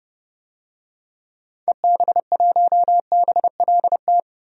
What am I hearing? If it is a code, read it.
Morse code, heard as a keyed tone, about 30 wpm, 700 Hz. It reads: EB1BLT
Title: DXCC-100 Call Signs - 6 Characters - Repeated 3x - 30wpm